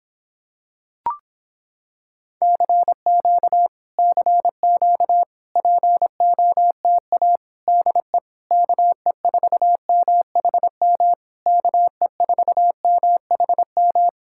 Morse code, heard as a keyed tone, about 26 wpm, 700 Hz. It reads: CQ CQ POTA DE KE4MHM KE4MHM